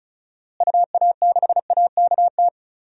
Morse code, heard as a keyed tone, about 36 wpm, 700 Hz. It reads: UA6AKT
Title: DXCC-100 Call Signs - 6 Characters - Repeated 3x - 35wpm